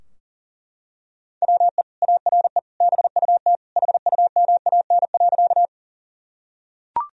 Morse code, heard as a keyed tone, about 40 wpm, 700 Hz. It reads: WE ARE BUT HUMAN.
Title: Sentences from Top 1000 Words - Review Part 3 - 40wpm